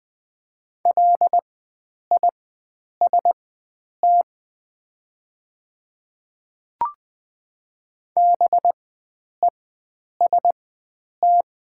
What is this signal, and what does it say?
Morse code, heard as a keyed tone, about 20 wpm, spaced as for 9 wpm, 700 Hz. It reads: LIST BEST
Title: 4x Character Spacing - Top 500 Words - 20wpm